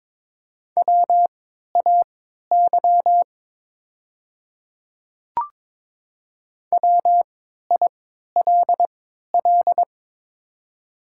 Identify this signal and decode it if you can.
Morse code, heard as a keyed tone, about 23 wpm, spaced as for 13 wpm, 700 Hz. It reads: WAY WILL
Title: 3x Character Spacing - Top 100 Words - 22wpm